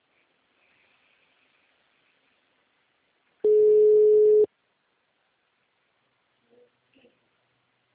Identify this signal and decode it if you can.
Ringback tone (Europe, 425 Hz).